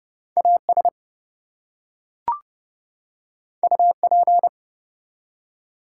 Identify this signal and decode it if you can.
Morse code, heard as a keyed tone, about 31 wpm, 700 Hz. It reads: AS UP
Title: Two Letter Words - 30wpm